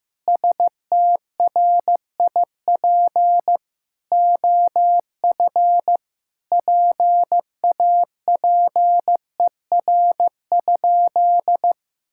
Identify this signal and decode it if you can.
Morse code, heard as a keyed tone, about 15 wpm, 700 Hz. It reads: STRIP OF PAPER?
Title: Sentences from Top 1600 Words 15wpm